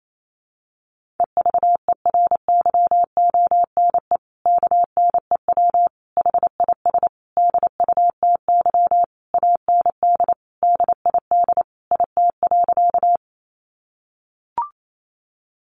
Morse code, heard as a keyed tone, about 28 wpm, 700 Hz. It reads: EVERYONE KNEW HIS DUTY AND DID IT.
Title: Sentences from Top 1500 Words 28wpm